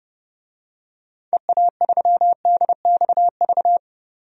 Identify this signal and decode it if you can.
Morse code, heard as a keyed tone, about 29 wpm, 700 Hz. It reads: EA3DXV
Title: DXCC-100 Call Signs - 6 Characters - 30wpm